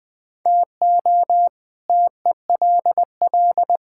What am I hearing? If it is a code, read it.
Morse code, heard as a keyed tone, about 20 wpm, 700 Hz. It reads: TO TELL
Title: Sentences with 5-Word Phrases - Spoken Beginning - 20wpm